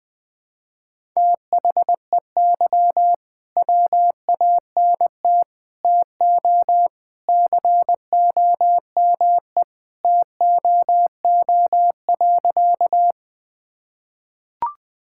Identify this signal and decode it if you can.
Morse code, heard as a keyed tone, about 20 wpm, 700 Hz. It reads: THEY WANT TO COME TOO.
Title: Sentences with 3-Word Phrases - 20wpm